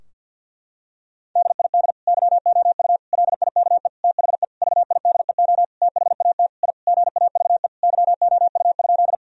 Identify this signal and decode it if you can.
Morse code, heard as a keyed tone, about 50 wpm, 700 Hz. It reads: DID YOU LIKE THE VIDEO THAT I GAVE YOU?
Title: Sentences from Top 1000 Words - Review Part 4 - 50wpm